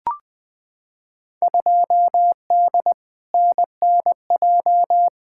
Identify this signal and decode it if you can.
Morse code, heard as a keyed tone, about 20 wpm, 700 Hz. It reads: 2D NNJ
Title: Field Day Exchanges - 20wpm